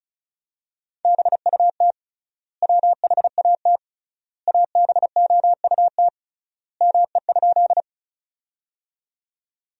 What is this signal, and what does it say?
Morse code, heard as a keyed tone, about 36 wpm, 700 Hz. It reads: BUT WHAT ABOUT ME?
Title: 3x Word-Spacing - Sentences from Top 200 Words - 35wpm